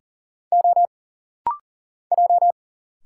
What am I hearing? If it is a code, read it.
Morse code, heard as a keyed tone, about 40 wpm, 700 Hz. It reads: O J
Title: Single Letter-Number - Rapid-Fire 40wpm